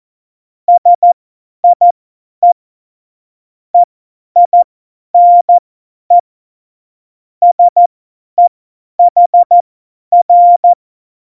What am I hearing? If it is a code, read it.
Morse code, heard as a keyed tone, about 14 wpm, spaced as for 10 wpm, 700 Hz. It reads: SIE EINE SEHR
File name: Morsekurs / Witze / Witz 06. Oktober 2020, 50 F70